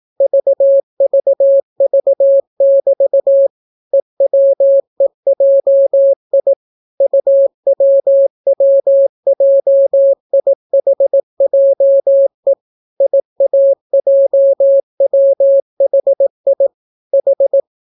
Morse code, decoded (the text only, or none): VVV<BT> EWEJI UWWJIHJE IAJWHI H